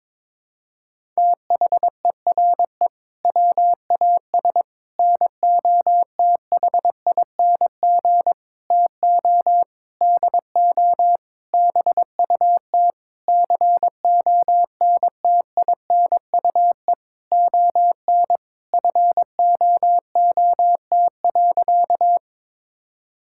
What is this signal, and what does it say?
Morse code, heard as a keyed tone, about 22 wpm, 700 Hz. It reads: THERE WAS NOTHING TO DO BUT CONTINUE ON FOOT.
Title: Sentences from Top 1000 Words - Review Part 6 - 22wpm